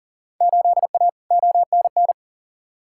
Morse code, heard as a keyed tone, about 40 wpm, 700 Hz. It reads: 8A ONN
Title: Field Day Exchanges Encore - 40wpm